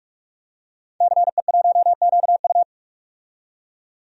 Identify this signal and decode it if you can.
Morse code, heard as a keyed tone, about 46 wpm, 700 Hz. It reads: CE1QU
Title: DXCC-100 Call Signs - 5 Characters - 45wpm